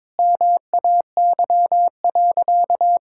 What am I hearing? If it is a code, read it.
Morse code, heard as a keyed tone, about 22 wpm, 700 Hz. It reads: MAY.
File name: practice / 22wpm / Sentences with 8-Word Phrases - 22wpm